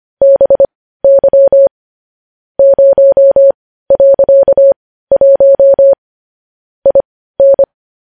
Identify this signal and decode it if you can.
Morse code, heard as a keyed tone, about 25 wpm, spaced as for 15 wpm, 550 Hz. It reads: BY 0.1 IN
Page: news-headlines-in-morse-code-at-15-wpm